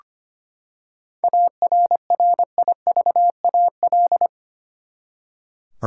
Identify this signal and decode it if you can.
Morse code, heard as a keyed tone, about 25 wpm, 700 Hz. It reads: ARRIVAL
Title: Seven Letter Words - 25wpm